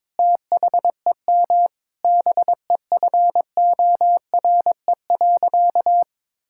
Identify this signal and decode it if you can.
Morse code, heard as a keyed tone, about 21 wpm, 700 Hz. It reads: THEM BEFORE.
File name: Sentences with 9-Word Phrases - 22wpm